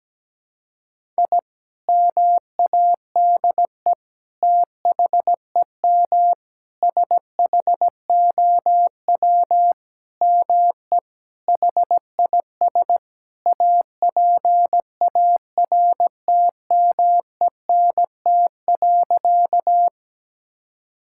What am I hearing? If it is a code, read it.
Morse code, heard as a keyed tone, about 17 wpm, 700 Hz. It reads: I MADE THEM SHOW ME HIS APARTMENT.
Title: Sentences from Top 1000 Words - Review Part 6 - 17wpm